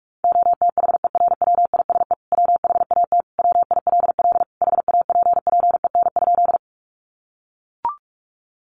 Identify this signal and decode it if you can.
Morse code, heard as a keyed tone, about 44 wpm, 700 Hz. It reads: OTHERWISE WHAT WILL HAPPEN?
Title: Sentences from Top 1200 Words 45wpm